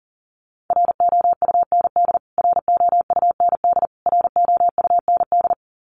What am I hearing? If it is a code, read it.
Morse code, heard as a keyed tone, about 40 wpm, 700 Hz. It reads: ROUND ROUND ROUND
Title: Top 300 Words Repeated 40wpm